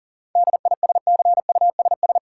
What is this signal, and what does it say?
Morse code, heard as a keyed tone, about 40 wpm, 700 Hz. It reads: DISCUSS
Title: Sentences from Top 1000 Words - New Word Drill - 40wpm